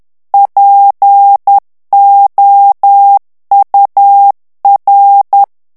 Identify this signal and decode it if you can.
Morse code, heard as a keyed tone, about 10 wpm, 800 Hz. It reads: POUR
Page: Français - texte en clair 1 - vitesse : 10 mots minute